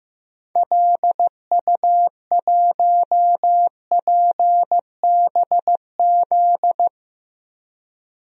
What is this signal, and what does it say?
Morse code, heard as a keyed tone, about 16 wpm, 700 Hz. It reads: LU1PBZ
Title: DXCC-100 Call Signs - 6 Characters - Repeated 2x - 15wpm